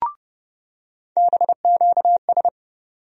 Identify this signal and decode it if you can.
Morse code, heard as a keyed tone, about 32 wpm, 700 Hz. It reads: BQS
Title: USA Call Sign Postfixes - 3 Letters 30wpm